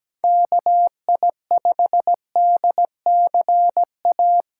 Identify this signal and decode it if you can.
Morse code, heard as a keyed tone, about 17 wpm, 700 Hz. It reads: KI5DCA
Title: USA Call Signs - 4x Character Spacing - 17wpm